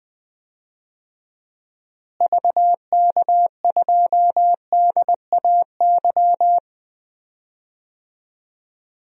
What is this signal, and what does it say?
Morse code, heard as a keyed tone, about 20 wpm, 700 Hz. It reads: VK2DAY
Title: DXCC-100 Call Signs - 6 Characters - Repeated 3x - 20wpm